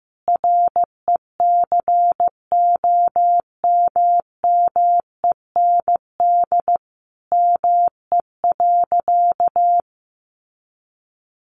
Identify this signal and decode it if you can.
Morse code, heard as a keyed tone, about 15 wpm, 700 Hz. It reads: RECOMMEND ME.
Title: Sentences from Top 1200 Words 15wpm